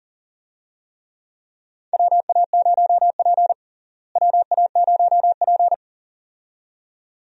Morse code, heard as a keyed tone, about 40 wpm, 700 Hz. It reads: WA0P WA0P
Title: USA Call Signs - 4 Characters - Repeated 2x - 40wpm